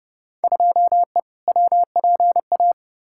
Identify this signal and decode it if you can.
Morse code, heard as a keyed tone, about 31 wpm, 700 Hz. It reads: 2E WPA
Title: Field Day Exchanges - 30wpm